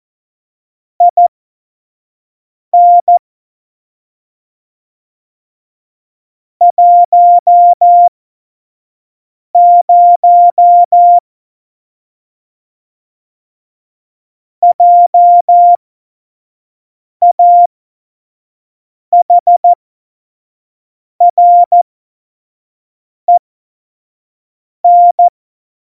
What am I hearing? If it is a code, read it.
Morse code, heard as a keyed tone, about 14 wpm, spaced as for 5 wpm, 700 Hz. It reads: IN 10 JAHREN
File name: Morsekurs / Witze / Witz 11 Juli 17, Tempo 25 F70